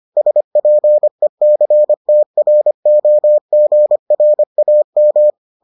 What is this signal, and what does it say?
Morse code, heard as a keyed tone, about 25 wpm, 600 Hz. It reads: SPECTROGRAM